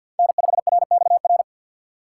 Morse code, heard as a keyed tone, about 48 wpm, 700 Hz. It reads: N5RXR
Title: USA Call Signs - 5 Characters - 4x Character Spacing - 50wpm